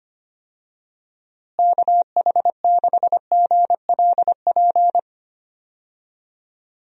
Morse code, heard as a keyed tone, about 25 wpm, 700 Hz. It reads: KH6GLP
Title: DXCC-100 Call Signs - 6 Characters - Repeated 2x - 25wpm